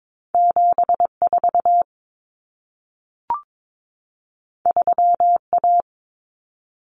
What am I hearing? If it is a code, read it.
Morse code, heard as a keyed tone, about 22 wpm, 700 Hz. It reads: 74 3A
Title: Common Cut Numbers - 2-Digit 22wpm